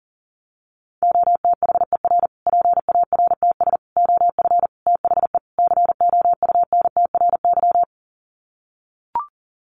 Morse code, heard as a keyed tone, about 40 wpm, 700 Hz. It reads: OTHER PARTS OF THE COUNTRY
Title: Sets of 5 Words Encore 40wpm